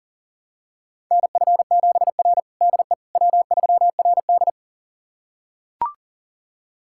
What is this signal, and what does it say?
Morse code, heard as a keyed tone, about 37 wpm, 700 Hz. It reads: NF7R DE W3RD
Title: USA Callsign DE Callsign - 4 Characters - 40wpm